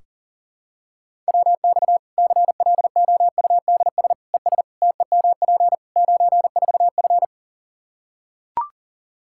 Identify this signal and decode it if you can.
Morse code, heard as a keyed tone, about 40 wpm, 700 Hz. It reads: WX CLOUDS ES TEMP 94F
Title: K1USN SST QSO Practice 40wpm